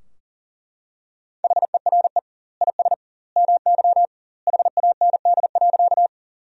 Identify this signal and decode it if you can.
Morse code, heard as a keyed tone, about 37 wpm, 700 Hz. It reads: HERE IS MY HAND.
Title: 2x Word-Spacing - Sentences from Top 200 Words - 40wpm